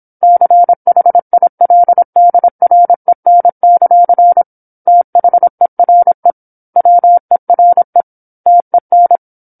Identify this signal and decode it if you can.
Morse code, heard as a keyed tone, about 27 wpm, 700 Hz. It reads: CHILDREN; THERE WERE TEN
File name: 08-Chapter-08 26 WPM